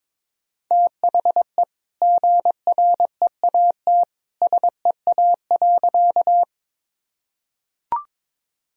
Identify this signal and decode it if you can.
Morse code, heard as a keyed tone, about 22 wpm, 700 Hz. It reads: THE GREAT SEA.
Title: Sentences from Top 1000 Words - Review Part 1 - 22wpm